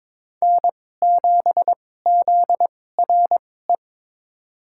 Morse code, heard as a keyed tone, about 21 wpm, spaced as for 16 wpm, 700 Hz. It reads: N7ZRE
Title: USA Call Signs - 5 Characters - 2x Character Spacing - 22wpm